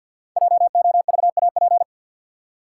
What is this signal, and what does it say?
Morse code, heard as a keyed tone, about 52 wpm, 700 Hz. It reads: JOVAP
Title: Code Groups - 5 Letters - 50wpm